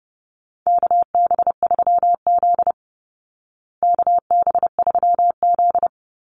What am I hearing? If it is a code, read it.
Morse code, heard as a keyed tone, about 30 wpm, 700 Hz. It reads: KB3Z KB3Z
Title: Call Signs Repeated 2x - 30wpm